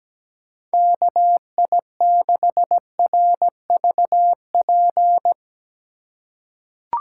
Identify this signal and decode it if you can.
Morse code, heard as a keyed tone, about 17 wpm, 700 Hz. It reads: KI6RVP
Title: USA Call Signs - 6 Characters - 4x Character Spacing - 17wpm